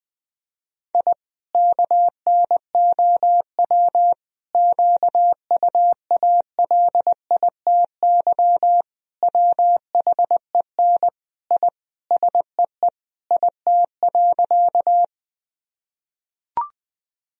Morse code, heard as a keyed tone, about 20 wpm, 700 Hz. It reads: I KNOW QUALITY WHEN I SEE IT.
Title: Sentences from Top 1000 Words - Review Part 4 - 20wpm